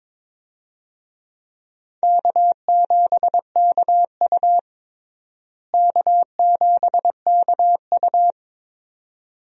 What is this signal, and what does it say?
Morse code, heard as a keyed tone, about 22 wpm, 700 Hz. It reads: K7KU K7KU
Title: USA Call Signs - 4 Characters - Repeated 2x - 22wpm